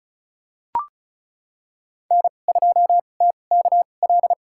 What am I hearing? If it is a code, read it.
Morse code, heard as a keyed tone, about 36 wpm, spaced as for 25 wpm, 700 Hz. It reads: N2TKL
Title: USA Call Signs - 5 Characters - 2x Character Spacing - 35wpm